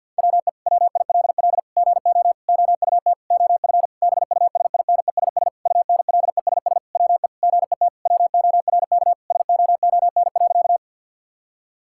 Morse code, heard as a keyed tone, about 50 wpm, 700 Hz. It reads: WE WILL GO OUT OF BUSINESS UNLESS WE GET WORK SOON.
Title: Sentences from Top 1000 Words - Review Part 6 - 50wpm